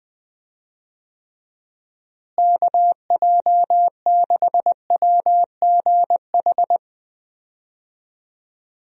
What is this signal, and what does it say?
Morse code, heard as a keyed tone, about 20 wpm, 700 Hz. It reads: KJ6WGH
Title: USA Call Signs - 6 Characters - Repeated 2x - 20wpm